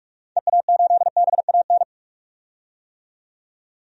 Morse code, heard as a keyed tone, about 41 wpm, 700 Hz. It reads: EA8BAN
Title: DXCC-100 Call Signs - 6 Characters - 45wpm